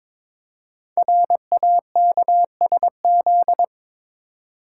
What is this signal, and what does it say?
Morse code, heard as a keyed tone, about 22 wpm, 700 Hz. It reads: RAKSZ
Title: Code Groups - 5 Letter:Numbers - 22wpm